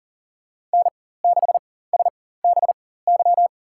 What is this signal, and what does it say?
Morse code, heard as a keyed tone, about 40 wpm, spaced as for 19 wpm, 700 Hz. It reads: N6SBY
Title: USA Call Signs - 5 Characters - 4x Character Spacing - 40wpm